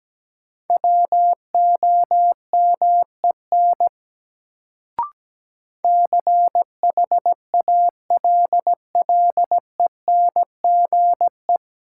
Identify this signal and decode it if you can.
Morse code, heard as a keyed tone, about 17 wpm, 700 Hz. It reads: WOMEN CHALLENGE
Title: Sentences from Top 1300 Words - New Word Drill Alt - 17wpm